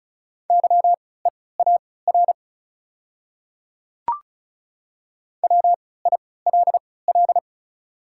Morse code, heard as a keyed tone, about 36 wpm, spaced as for 20 wpm, 700 Hz. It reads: YEAR WILL
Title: 3x Character Spacing - Top 200 Words - 35wpm